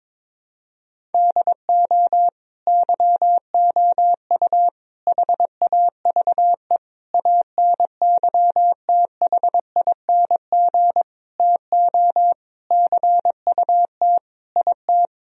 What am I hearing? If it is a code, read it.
Morse code, heard as a keyed tone, about 22 wpm, 700 Hz. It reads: DO YOU HAVE ANYTHING TO CUT IT
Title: Sentences from Top 1000 Words - Review Part 1 - 22wpm